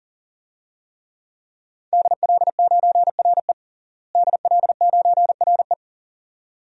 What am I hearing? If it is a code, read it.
Morse code, heard as a keyed tone, about 37 wpm, 700 Hz. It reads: DL9RE DL9RE
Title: DXCC-100 Call Signs - 5 Characters - Repeated 2x - 40wpm